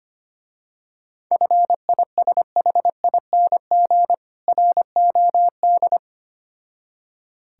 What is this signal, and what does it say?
Morse code, heard as a keyed tone, about 25 wpm, 700 Hz. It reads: FISHING ROD
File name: practice / 25wpm / Difficult Words with Hints - 25wpm